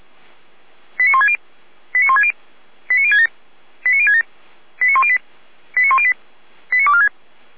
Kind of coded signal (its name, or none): five-tone selective call